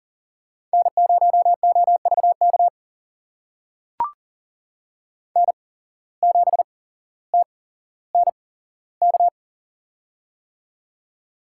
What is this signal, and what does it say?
Morse code, heard as a keyed tone, about 40 wpm, spaced as for 17 wpm, 700 Hz. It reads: N0OVK N7TNK
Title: USA Call Signs - 5 Characters - 8x Character Spacing - 40wpm